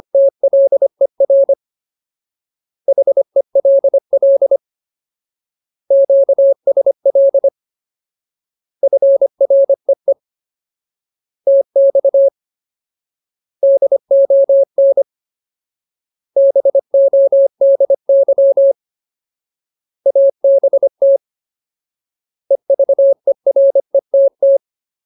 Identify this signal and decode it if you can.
Morse code, heard as a keyed tone, about 25 wpm, 550 Hz. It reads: TLER HELL QSL FREE TX DON BODY ABT EVERETT
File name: Ultimate Word List / 25WPM Ultimate